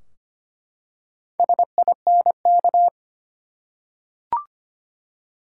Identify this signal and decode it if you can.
Morse code, heard as a keyed tone, about 25 wpm, 700 Hz. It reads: SINK
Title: Four Letter Words - 25wpm